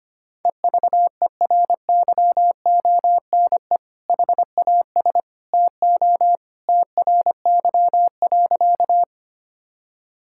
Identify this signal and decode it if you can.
Morse code, heard as a keyed tone, about 25 wpm, 700 Hz. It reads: EVERYONE HAS TO TRY.
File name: practice / 25wpm / Sentences with 2-Word Phrases - 25wpm